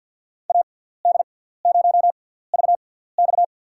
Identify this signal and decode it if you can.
Morse code, heard as a keyed tone, about 50 wpm, spaced as for 17 wpm, 700 Hz. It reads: AD0VX
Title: USA Call Signs - 5 Characters - 6x Character Spacing - 50wpm